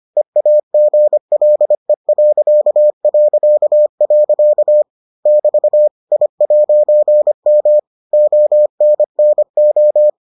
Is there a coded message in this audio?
Yes: Morse code from 0.2 to 10.1 s, heard as a keyed tone, about 25 wpm, 600 Hz. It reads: EAGLE... <BT> I'M ONNO